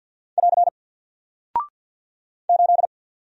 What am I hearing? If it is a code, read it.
Morse code, heard as a keyed tone, about 49 wpm, 700 Hz. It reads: <AR> 8
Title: Single Character (Letter:Number:Punctuation:Prosign) Alt 50wpm